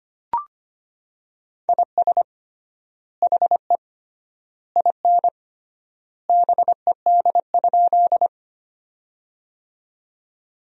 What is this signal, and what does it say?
Morse code, heard as a keyed tone, about 25 wpm, 700 Hz. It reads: IS HE IN BED?
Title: 3x Word-Spacing - Sentences from Top 500 Words - 25wpm